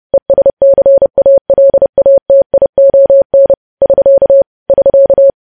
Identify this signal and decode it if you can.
Morse code, heard as a keyed tone, about 30 wpm, 550 Hz. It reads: ESCALATION <SK> <SK>
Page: News Headlines in Morse Code at 30 WPM